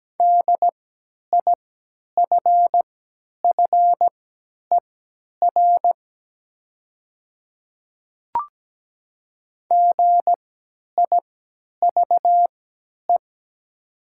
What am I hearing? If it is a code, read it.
Morse code, heard as a keyed tone, about 17 wpm, spaced as for 10 wpm, 700 Hz. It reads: DIFFER GIVE
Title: 3x Character Spacing - Top 300 Words - 17wpm